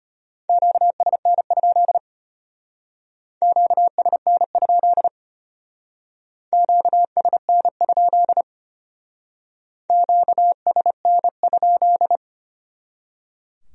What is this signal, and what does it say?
Morse code, heard as a keyed tone, about 32 wpm, 700 Hz. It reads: QSN? QSN? QSN? QSN?